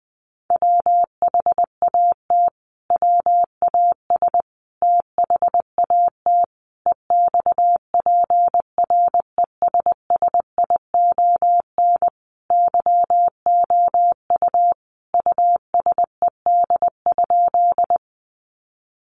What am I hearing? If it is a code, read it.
Morse code, heard as a keyed tone, about 20 wpm, 700 Hz. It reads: WHAT WAS THAT EXPRESSION YOU USED?
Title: Sentences from Top 1200 Words 20wpm